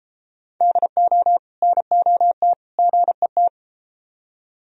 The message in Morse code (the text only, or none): DO NOT GET